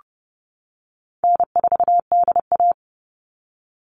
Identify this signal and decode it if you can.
Morse code, heard as a keyed tone, about 30 wpm, 700 Hz. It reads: N4DA
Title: NA SOTA Chaser Callsigns 30wpm